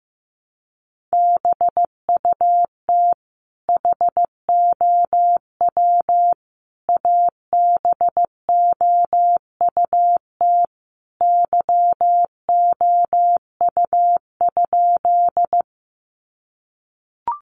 Morse code, heard as a keyed tone, about 15 wpm, 700 Hz. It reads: BUT HOW ABOUT YOU?
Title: Sentences from Top 100 Words 15wpm